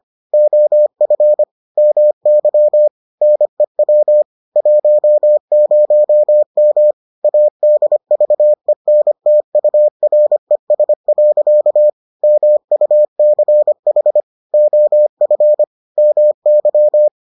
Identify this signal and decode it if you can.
Morse code, heard as a keyed tone, about 25 wpm, 600 Hz. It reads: OF MY NEW 10M ADVENTURES. MUCH OF MY